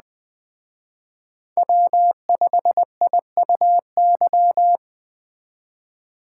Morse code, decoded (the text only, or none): W5IUY